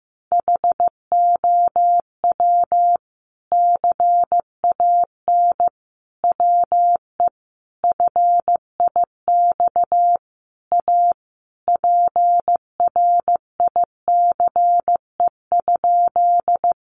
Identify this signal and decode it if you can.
Morse code, heard as a keyed tone, about 15 wpm, 700 Hz. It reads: HOW CAN WE FIX A PRICE?
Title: Sentences from Top 1400 Words 15wpm